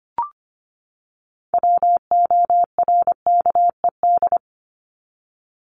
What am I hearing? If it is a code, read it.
Morse code, heard as a keyed tone, about 25 wpm, 700 Hz. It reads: WORKED
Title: ED Suffix - Top 100 Words with Base Words 25wpm